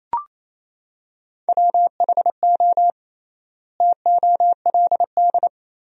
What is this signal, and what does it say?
Morse code, heard as a keyed tone, about 28 wpm, 700 Hz. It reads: WHO TOLD
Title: 3x Word-Spacing - Sentences from Top 400 Words - 28wpm